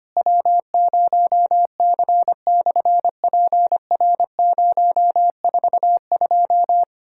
Morse code, heard as a keyed tone, about 24 wpm, 700 Hz. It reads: W0C/PR042
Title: SOTA QSO Practice 25wpm